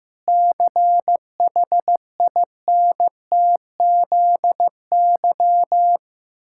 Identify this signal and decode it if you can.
Morse code, heard as a keyed tone, about 15 wpm, 700 Hz. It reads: CHINTZY
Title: Pangram Challenge v2 - 15wpm